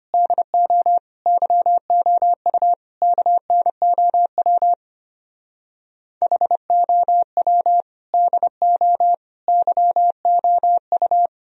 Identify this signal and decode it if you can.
Morse code, heard as a keyed tone, about 32 wpm, 690 Hz. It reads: DO YOU KNOW HOW DO YOU